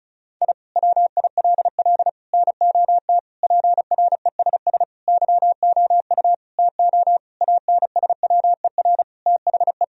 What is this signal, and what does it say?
Morse code, heard as a keyed tone, about 36 wpm, 700 Hz. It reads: I WILL NOT PRESS YOU TO ANSWER THE
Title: Sentences from Top 1000 Words - Review Part 1 - 35wpm